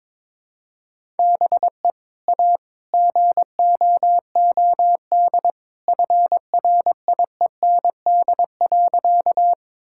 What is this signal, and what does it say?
Morse code, heard as a keyed tone, about 22 wpm, 700 Hz. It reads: BE A GOOD FRIEND.
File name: practice / 22wpm / Sentences with 2-Word Phrases - 22wpm